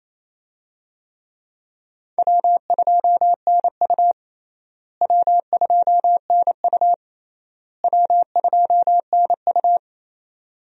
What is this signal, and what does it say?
Morse code, heard as a keyed tone, about 28 wpm, 700 Hz. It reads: W2NU W2NU W2NU
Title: USA Call Signs - 4 Characters - Repeated 3x - 28wpm